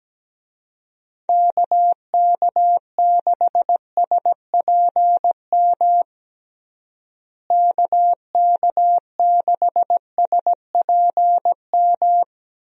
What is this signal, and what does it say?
Morse code, heard as a keyed tone, about 17 wpm, 700 Hz. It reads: KK6SPM KK6SPM
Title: USA Call Signs - 6 Characters - Repeated 2x - 17wpm